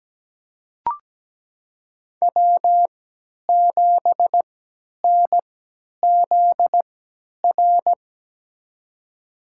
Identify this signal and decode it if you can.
Morse code, heard as a keyed tone, about 17 wpm, spaced as for 10 wpm, 700 Hz. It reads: W7NZR
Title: USA Call Signs - 5 Characters - 3x Character Spacing - 17wpm